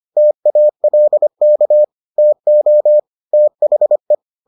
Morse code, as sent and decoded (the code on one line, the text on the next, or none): - .- .-.. -.- / - --- / - .... .
TALK TO THE